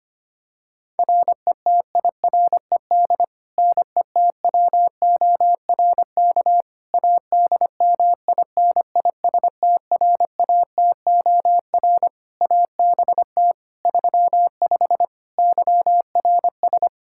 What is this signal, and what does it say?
Morse code, heard as a keyed tone, about 25 wpm, 700 Hz. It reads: RETIRED NETWORK ADMINISTRATOR ABT 35 YRS
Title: Practice QSO - Exchange 3 - More Info - 25wpm